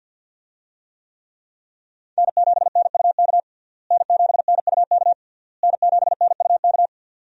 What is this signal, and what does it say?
Morse code, heard as a keyed tone, about 50 wpm, 700 Hz. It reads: N7NUK N7NUK N7NUK
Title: USA Call Signs - 5 Characters - Repeated 3x - 50wpm